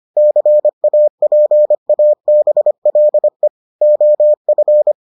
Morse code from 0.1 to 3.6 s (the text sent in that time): CAPABLE